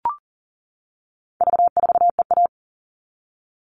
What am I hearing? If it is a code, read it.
Morse code, heard as a keyed tone, about 40 wpm, 700 Hz. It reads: V4EA
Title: All Cut Numbers - 4-Digit 40wpm